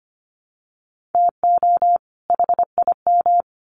Morse code, heard as a keyed tone, about 25 wpm, 700 Hz. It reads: TO HIM
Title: Sets of 2 Words 25wpm